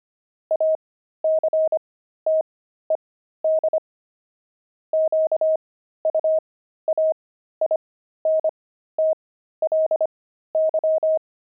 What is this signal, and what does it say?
Morse code, heard as a keyed tone, about 25 wpm, spaced as for 13 wpm, 630 Hz. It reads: ACTED QUAINTLY
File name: pangram-fs-13-cs-25-jaded